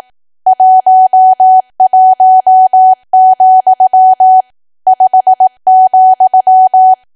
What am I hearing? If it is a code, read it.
Morse code, heard as a keyed tone, about 18 wpm, 750 Hz. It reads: 11, 5,
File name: cwa / W1AW-Level2 / 18WPM